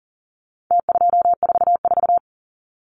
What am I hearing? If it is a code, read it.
Morse code, heard as a keyed tone, about 40 wpm, 700 Hz. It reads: T244
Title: Numbers - Contest Style - 4-Digit 40wpm